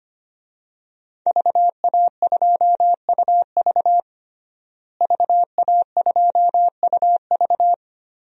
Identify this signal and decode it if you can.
Morse code, heard as a keyed tone, about 25 wpm, 700 Hz. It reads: VA2UV VA2UV
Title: DXCC-100 Call Signs - 5 Characters - Repeated 2x - 25wpm